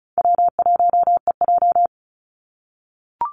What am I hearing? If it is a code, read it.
Morse code, heard as a keyed tone, about 34 wpm, 700 Hz. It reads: W1EJ
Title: NA SOTA Activator Callsigns 35wpm